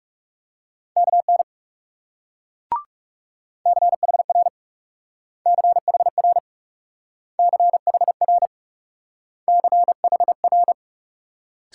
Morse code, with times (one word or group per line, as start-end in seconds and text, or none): 1.0-1.4 KN
3.7-4.5 CHR
5.5-6.4 CHR
7.4-8.5 CHR
9.5-10.7 CHR